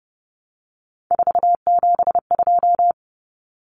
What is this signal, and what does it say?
Morse code, heard as a keyed tone, about 30 wpm, 700 Hz. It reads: 472
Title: Numbers - 3-Digit 30wpm